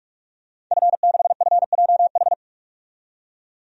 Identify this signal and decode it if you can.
Morse code, heard as a keyed tone, about 42 wpm, 700 Hz. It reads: F6FJH
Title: DXCC-100 Call Signs - 5 Characters - 45wpm